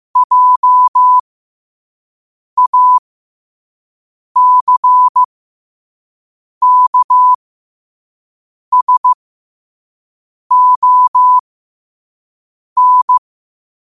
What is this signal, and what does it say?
Morse code, heard as a keyed tone, about 15 wpm, spaced as for 5 wpm, 1000 Hz. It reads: JACKSON
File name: qotdfiles / 5wpm